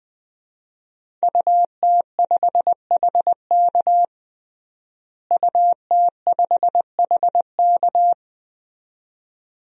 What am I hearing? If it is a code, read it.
Morse code, heard as a keyed tone, about 20 wpm, 700 Hz. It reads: UT5HK UT5HK
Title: DXCC-100 Call Signs - 5 Characters - Repeated 2x - 20wpm